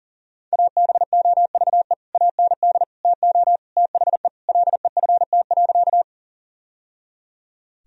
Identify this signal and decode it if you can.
Morse code, heard as a keyed tone, about 40 wpm, 700 Hz. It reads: ABOVE AND TO THE LEFT.
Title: Sentences from Top 1000 Words - Review Part 2 - 40wpm